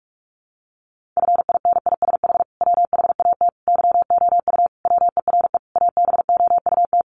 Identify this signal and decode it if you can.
Morse code, heard as a keyed tone, about 44 wpm, 700 Hz. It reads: FINISH WHAT YOU WERE ABOUT
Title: Sentences from Top 1200 Words 45wpm